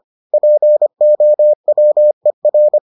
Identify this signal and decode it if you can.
Morse code, heard as a keyed tone, about 25 wpm, 600 Hz. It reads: POWER